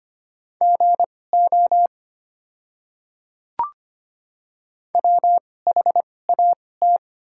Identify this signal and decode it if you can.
Morse code, heard as a keyed tone, about 25 wpm, spaced as for 18 wpm, 700 Hz. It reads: GO WHAT
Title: 2x Character Spacing - Top 100 Words - 25wpm